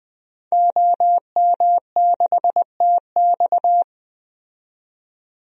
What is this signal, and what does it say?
Morse code, heard as a keyed tone, about 20 wpm, 700 Hz. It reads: OM6TX
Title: DXCC-100 Call Signs - 5 Characters - 20wpm